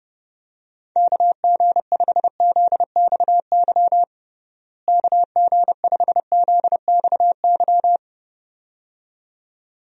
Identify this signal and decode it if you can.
Morse code, heard as a keyed tone, about 30 wpm, 700 Hz. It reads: KG5ZXY KG5ZXY
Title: USA Call Signs - 6 Characters - Repeated 2x - 30wpm